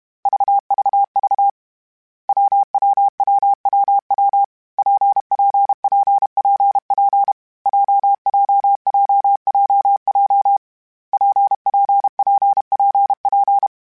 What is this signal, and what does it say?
Morse code, heard as a keyed tone, about 32 wpm, 800 Hz. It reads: VVV WWWWW PPPPP JJJJJ PPPPP
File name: Difficult Combos / Difficult Combos-mp3s / WPJ WPJ1 @30